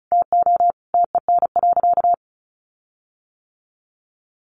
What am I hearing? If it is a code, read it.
Morse code, heard as a keyed tone, about 34 wpm, 700 Hz. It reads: TO TEN.
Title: Sentences from Top 1300 Words 35wpm